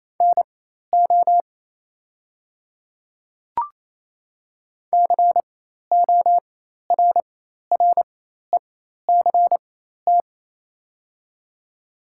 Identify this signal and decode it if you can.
Morse code, heard as a keyed tone, about 28 wpm, spaced as for 13 wpm, 700 Hz. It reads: NO CORRECT
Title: 4x Character Spacing - Top 500 Words - 28wpm